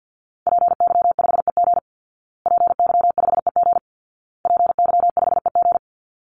Morse code, heard as a keyed tone, about 50 wpm, 700 Hz. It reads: PY5EG PY5EG PY5EG
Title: Call Signs Repeated 2x - 50wpm